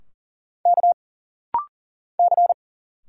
Morse code, heard as a keyed tone, about 40 wpm, 700 Hz. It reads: K C
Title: Single Letters - Rapid-Fire 40wpm